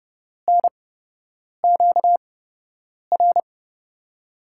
Morse code, heard as a keyed tone, about 30 wpm, spaced as for 8 wpm, 700 Hz. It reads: NQR
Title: USA Call Signs - 6 Characters - 8x Character Spacing - 30wpm